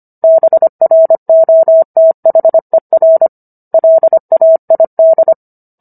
Morse code, heard as a keyed tone, about 25 wpm, 650 Hz. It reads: BROTHER LAID